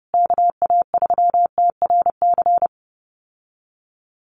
Morse code, heard as a keyed tone, about 30 wpm, 700 Hz. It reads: KA3TRC
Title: USA Call Signs Repeated 3x 30wpm